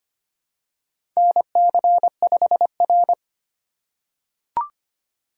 Morse code, heard as a keyed tone, about 25 wpm, 700 Hz. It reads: NC5R
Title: USA Call Signs - 4 Characters - 3x Character Spacing - 25wpm